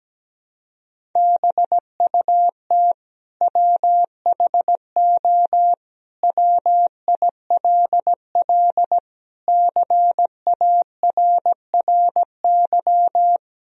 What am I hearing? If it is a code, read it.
Morse code, heard as a keyed tone, about 17 wpm, 700 Hz. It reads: BUT WHO WILL CARRY
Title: Sentences from Top 1000 Words - Review Part 1 - 17wpm